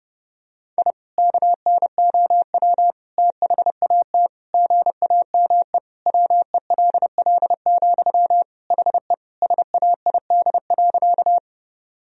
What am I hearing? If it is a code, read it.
Morse code, heard as a keyed tone, about 30 wpm, 700 Hz. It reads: I KNOW THAT GAME WELL, HE SAID.
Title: Sentences from Top 1000 Words - Review Part 3 - 30wpm